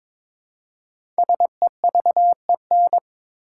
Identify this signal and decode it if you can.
Morse code, heard as a keyed tone, about 22 wpm, 700 Hz. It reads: SEVEN
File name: 5 Letter Words - First 3 Letters Given - 22wpm